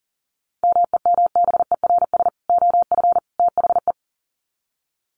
Morse code, heard as a keyed tone, about 40 wpm, 700 Hz. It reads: MEMBERS OF THE
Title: Sets of 3 Words 40wpm